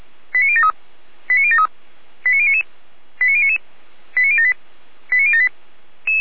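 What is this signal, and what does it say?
Five-tone selective call (ZVEI), heard as quick runs of five tones, tones of about 70 ms. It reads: 89073 89073 89099 89099 89078 89078
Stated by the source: ZVEI five-tone call received by SDR.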